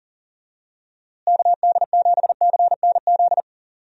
Morse code, heard as a keyed tone, about 37 wpm, 700 Hz. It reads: KD7CNZ
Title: USA Call Signs - 6 Characters - Repeated 2x - 40wpm